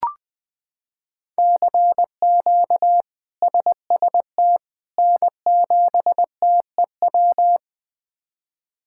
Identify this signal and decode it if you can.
Morse code, heard as a keyed tone, about 20 wpm, 700 Hz. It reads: CQ SST N7TEW
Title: K1USN SST QSO Practice - 20wpm